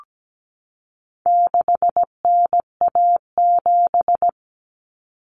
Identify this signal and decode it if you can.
Morse code, heard as a keyed tone, about 17 wpm, 700 Hz. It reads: 6NA7
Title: Common Cut Numbers - 4-Digit 17wpm